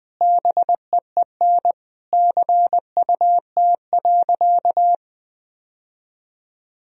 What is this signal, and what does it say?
Morse code, heard as a keyed tone, about 20 wpm, 700 Hz. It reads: BEEN CUT.
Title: Sentences from Top 1000 Words - Review Part 1 - 20wpm